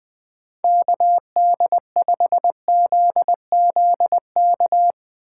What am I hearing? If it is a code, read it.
Morse code, heard as a keyed tone, about 21 wpm, 700 Hz. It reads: KD5ZZK
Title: 2023 SOTA Activator Call Signs - 20wpm